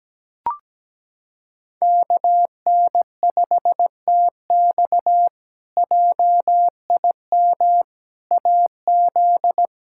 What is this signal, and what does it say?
Morse code, heard as a keyed tone, about 17 wpm, 700 Hz. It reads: KN5TX JIM AZ
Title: K1USN SST QSO Practice - 17wpm